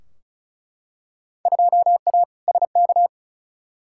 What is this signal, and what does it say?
Morse code, heard as a keyed tone, about 36 wpm, 700 Hz. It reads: 2A SK
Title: Field Day Exchanges Encore - 35wpm